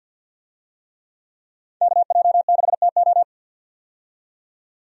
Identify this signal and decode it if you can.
Morse code, heard as a keyed tone, about 51 wpm, 700 Hz. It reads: KJ6TO
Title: USA Call Signs - 5 Characters - 50wpm